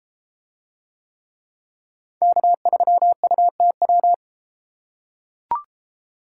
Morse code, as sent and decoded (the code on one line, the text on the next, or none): -.- ...-- ..- - .--
K3UTW